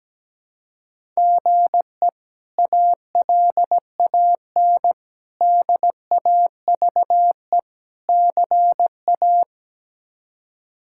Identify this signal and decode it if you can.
Morse code, heard as a keyed tone, about 17 wpm, 700 Hz. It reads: GE ALAN DAVE CA
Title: K1USN SST QSO Practice - 17wpm